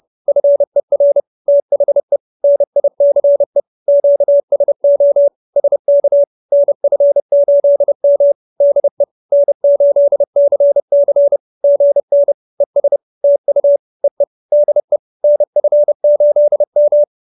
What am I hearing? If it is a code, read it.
Morse code, heard as a keyed tone, about 30 wpm, 570 Hz. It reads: FER THE NICE QSO SK NF8M DE N8CC GN ES TU EE DE NF8M